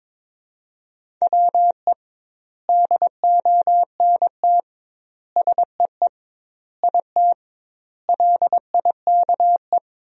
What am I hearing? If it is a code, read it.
Morse code, heard as a keyed tone, about 22 wpm, 700 Hz. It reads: WE DONT SEE IT LIKE
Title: 2x Word-Spacing - Sentences from Top 300 Words - 22wpm